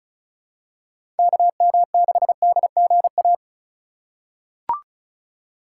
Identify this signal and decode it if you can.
Morse code, heard as a keyed tone, about 36 wpm, 700 Hz. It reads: KM6DGA
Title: USA Call Signs - 6 Characters - 2x Character Spacing - 35wpm